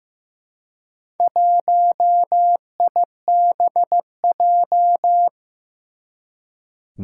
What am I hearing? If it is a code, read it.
Morse code, heard as a keyed tone, about 15 wpm, 700 Hz. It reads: 1IBJ
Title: Four Challenging Letter-Number Combinations 15wpm